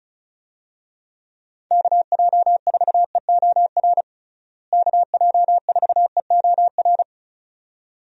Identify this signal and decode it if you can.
Morse code, heard as a keyed tone, about 36 wpm, 700 Hz. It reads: KJ4EOR KJ4EOR
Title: USA Call Signs - 6 Characters - Repeated 2x - 35wpm